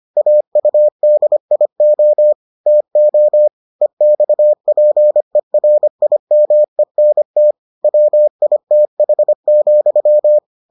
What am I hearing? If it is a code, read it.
Morse code, heard as a keyed tone, about 25 wpm, 600 Hz. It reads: AUDIO TO EXPERIMENT WITH,